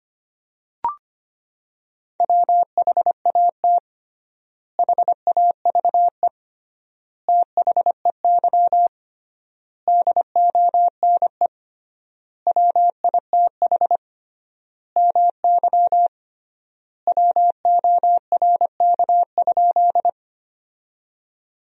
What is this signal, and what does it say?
Morse code, heard as a keyed tone, about 25 wpm, 700 Hz. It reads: WHAT HAVE THEY DONE WITH MY WORK?
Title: 3x Word-Spacing - Sentences from Top 500 Words - 25wpm